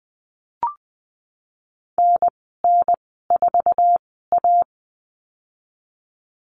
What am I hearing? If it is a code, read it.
Morse code, heard as a keyed tone, about 20 wpm, spaced as for 14 wpm, 700 Hz. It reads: NN4A
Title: 048 Lesson 9g - Number 4 - Call Signs 20wpm